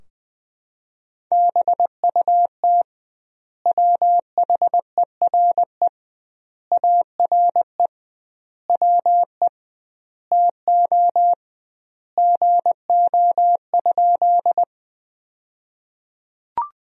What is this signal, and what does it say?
Morse code, heard as a keyed tone, about 20 wpm, 700 Hz. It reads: BUT WHERE ARE WE TO GO?
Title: 2x Word-Spacing - Sentences from Top 200 Words - 20wpm